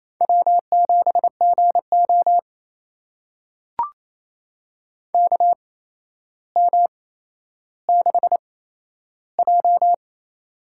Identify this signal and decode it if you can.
Morse code, heard as a keyed tone, about 28 wpm, spaced as for 12 wpm, 700 Hz. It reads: W7GO KM6J
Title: USA Call Signs - 8x Character Spacing - 28wpm